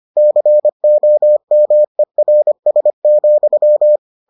Morse code, heard as a keyed tone, about 25 wpm, 600 Hz. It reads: COMERS,